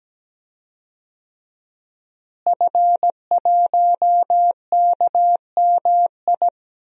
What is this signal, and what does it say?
Morse code, heard as a keyed tone, about 17 wpm, 700 Hz. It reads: F1KMI
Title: DXCC-100 Call Signs - 5 Characters - Repeated 2x - 17wpm